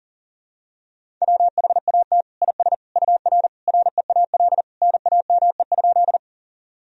Morse code, heard as a keyed tone, about 40 wpm, 700 Hz. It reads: WHAT IS UR REAL NAME?
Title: Sentences from Top 300 Words and Abbreviations 40wpm